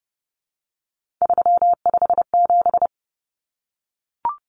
Morse code, heard as a keyed tone, about 30 wpm, 700 Hz. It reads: 357
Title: Common Cut Numbers - 3-Digit 30wpm